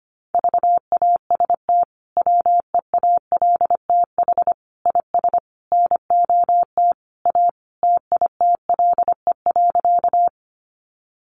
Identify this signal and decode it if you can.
Morse code, heard as a keyed tone, about 25 wpm, 700 Hz. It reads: VAST WEALTH IS NOT A TITLE.
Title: Sentences from Top 1300 Words 25wpm